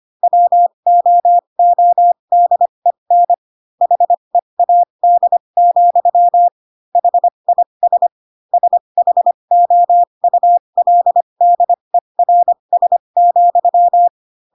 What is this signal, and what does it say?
Morse code, heard as a keyed tone, about 25 wpm, spaced as for 22 wpm, 700 Hz. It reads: WOODEN HEAD, HIS SHOULDERS,
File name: Pinocchio / Pinocchio ch6 18 25 ik1hge